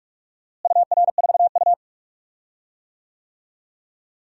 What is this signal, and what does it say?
Morse code, heard as a keyed tone, about 46 wpm, 700 Hz. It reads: UR4U
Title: DXCC-100 Call Signs - 4 Characters - Repeated 2x - 45wpm